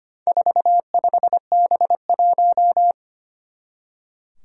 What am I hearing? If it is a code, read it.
Morse code, heard as a keyed tone, about 25 wpm, 700 Hz. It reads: 45B1
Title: Four Challenging Letter-Number Combinations 25wpm